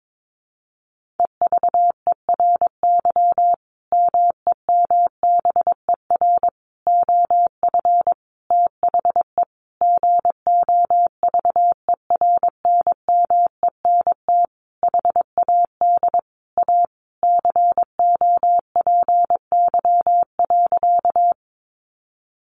Morse code, heard as a keyed tone, about 22 wpm, 700 Hz. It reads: EVERY MEMBER OF THE GOVERNMENT HAD A COPY.
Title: Sentences from Top 1400 Words 22wpm